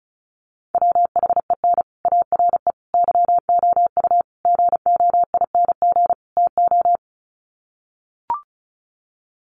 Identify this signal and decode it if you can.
Morse code, heard as a keyed tone, about 34 wpm, 700 Hz. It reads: WHEN ARE YOU GOING TO
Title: Sets of 5 Words Encore2 35wpm